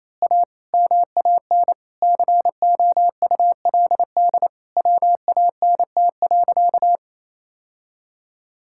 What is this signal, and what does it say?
Morse code, heard as a keyed tone, about 28 wpm, 700 Hz. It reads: A MAN COULD WANT.
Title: Sentences from Top 1000 Words - Review Part 5 - 28wpm